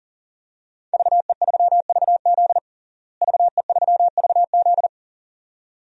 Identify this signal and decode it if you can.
Morse code, heard as a keyed tone, about 40 wpm, 700 Hz. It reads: VE3VZ VE3VZ
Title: DXCC-100 Call Signs - 5 Characters - Repeated 2x - 40wpm